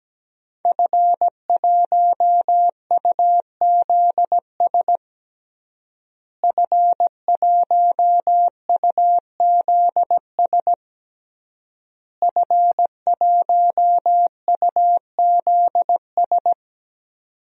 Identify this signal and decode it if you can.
Morse code, heard as a keyed tone, about 17 wpm, 700 Hz. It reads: F1UZS F1UZS F1UZS
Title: DXCC-100 Call Signs - 5 Characters - Repeated 3x - 17wpm